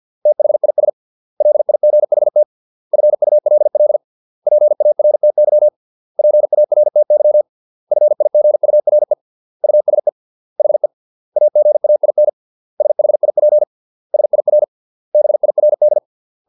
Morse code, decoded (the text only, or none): THIS LIGHT FULL PARTY PARTY FIGURE USE HE AGAIN SHIP SIR BIRD